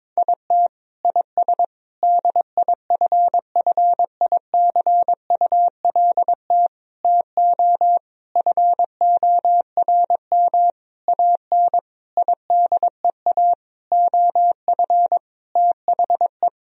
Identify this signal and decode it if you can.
Morse code, heard as a keyed tone, about 21 wpm, 700 Hz. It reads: IT IS DIFFICULT TO FORM AN IDEA OF THE
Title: Sentences from Top 1000 Words - Review Part 1 - 22wpm